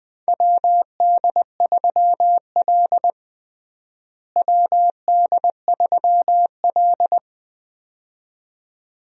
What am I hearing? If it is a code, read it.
Morse code, heard as a keyed tone, about 20 wpm, 700 Hz. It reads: WD3L WD3L
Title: USA Call Signs - 4 Characters - Repeated 2x - 20wpm